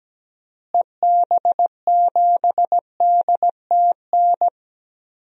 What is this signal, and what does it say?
Morse code, heard as a keyed tone, about 17 wpm, 700 Hz. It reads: EB7DTN
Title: DXCC-100 Call Signs - 6 Characters - Repeated 2x - 17wpm